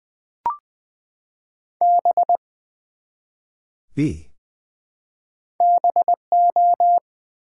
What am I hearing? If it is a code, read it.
Morse code, heard as a keyed tone, about 20 wpm, 700 Hz. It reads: B BO
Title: Build A Word - 1 Letter At A Time - 4-Letter Words - 20wpm